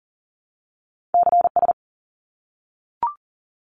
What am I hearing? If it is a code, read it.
Morse code, heard as a keyed tone, about 39 wpm, 700 Hz. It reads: CS
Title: Abbreviations 40wpm